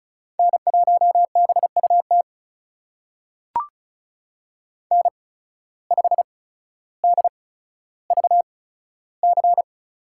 Morse code, heard as a keyed tone, about 36 wpm, spaced as for 15 wpm, 690 Hz. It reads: N1BUT N5DVC
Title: USA Call Signs - 5 Characters - 8x Character Spacing - 35wpm